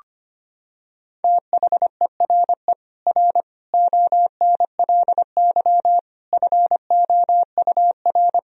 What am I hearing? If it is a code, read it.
Morse code, heard as a keyed tone, about 25 wpm, 700 Hz. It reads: THERE R ONLY FOUR
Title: Sentences from Top 300 Words and Abbreviations 25wpm